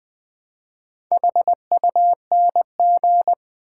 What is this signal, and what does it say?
Morse code, heard as a keyed tone, about 20 wpm, 690 Hz. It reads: HUNG
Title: Build A Word - 2 Letters At A Time - 6-Letter Words - 20wpm